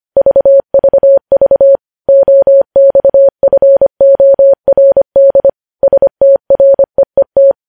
Morse code, heard as a keyed tone, about 25 wpm, 550 Hz. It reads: VVV OXFORD STREET
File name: podcast / News headlines-1x-25x25